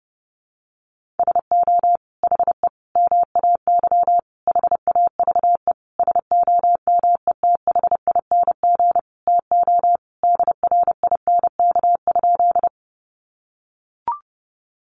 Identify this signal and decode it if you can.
Morse code, heard as a keyed tone, about 30 wpm, 700 Hz. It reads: SO HE MAY HAVE SOMETHING TO DRINK?
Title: Sentences from Top 1500 Words 30wpm